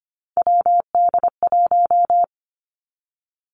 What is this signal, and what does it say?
Morse code, heard as a keyed tone, about 25 wpm, 700 Hz. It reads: WD1
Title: Three Challenging Letter-Number Combinations 25wpm